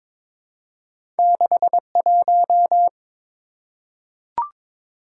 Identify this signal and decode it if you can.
Morse code, heard as a keyed tone, about 22 wpm, 690 Hz. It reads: 61
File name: practice / 22wpm / Two Letter-Number Combinations - 22wpm